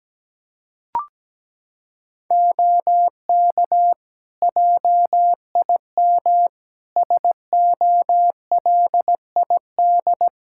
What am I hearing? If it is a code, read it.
Morse code, heard as a keyed tone, about 17 wpm, 700 Hz. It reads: OK JIM SOLID
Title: Practice QSO - Exchange 3 - More Info - 17wpm